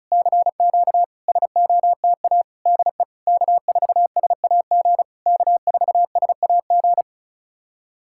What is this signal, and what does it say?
Morse code, heard as a keyed tone, about 36 wpm, 700 Hz. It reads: CQ SOTA DE K4SAG K4SAG
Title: SOTA QSO Practice 35wpm